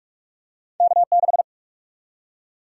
Morse code, heard as a keyed tone, about 46 wpm, 700 Hz. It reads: K6
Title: USA Call Sign Prefixes - 45wpm